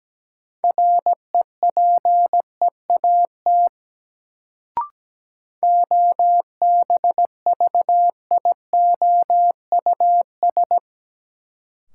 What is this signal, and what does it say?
Morse code, heard as a keyed tone, about 17 wpm, 700 Hz. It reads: REPEAT OBVIOUS
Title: Sentences from Top 1600 Words - New Word Drill - 17wpm